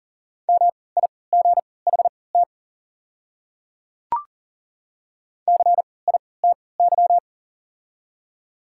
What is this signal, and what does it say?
Morse code, heard as a keyed tone, about 38 wpm, spaced as for 22 wpm, 690 Hz. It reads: MIGHT CITY
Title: 3x Character Spacing - Top 300 Words - 40wpm